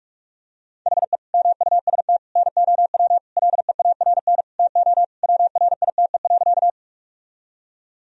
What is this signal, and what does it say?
Morse code, heard as a keyed tone, about 45 wpm, 700 Hz. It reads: HE MUST NOW LEARN TO WRITE.
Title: Sentences from Top 1000 Words - Review Part 1 - 45wpm